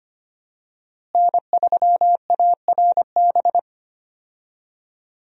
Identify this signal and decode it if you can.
Morse code, heard as a keyed tone, about 25 wpm, 700 Hz. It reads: N3ARB
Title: USA Call Signs - 5 Characters - Repeated 2x - 25wpm